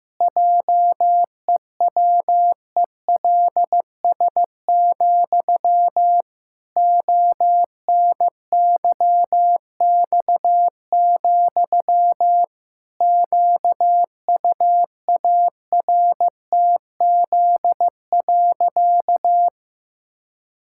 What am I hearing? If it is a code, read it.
Morse code, heard as a keyed tone, about 15 wpm, 700 Hz. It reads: JEWELS, ONYX, QUARTZ.
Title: Pangram Challenge v2 - 15wpm